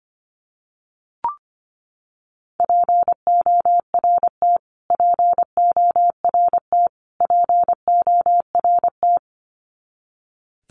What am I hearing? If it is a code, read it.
Morse code, heard as a keyed tone, about 25 wpm, 700 Hz. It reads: PORT PORT PORT
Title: Top 200 Words Repeated 25wpm